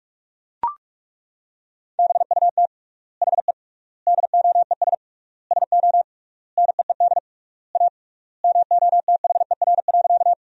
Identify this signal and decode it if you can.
Morse code, heard as a keyed tone, about 41 wpm, 700 Hz. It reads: BUT HE DOES SO NEED A MOTHER.
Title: 3x Word-Spacing - Sentences from Top 200 Words - 45wpm